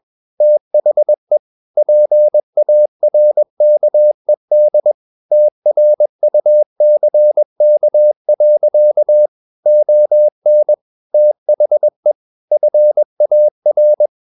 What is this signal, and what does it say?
Morse code, heard as a keyed tone, about 21 wpm, 600 Hz. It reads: THE PARKED TRUCK. ON THE FAR
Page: Romanen Never af Ken Follet på engelsk.